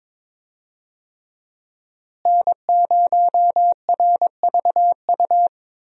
Morse code, heard as a keyed tone, about 22 wpm, 700 Hz. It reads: N0RVU
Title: USA Call Signs - 5 Characters - Repeated 3x - 22wpm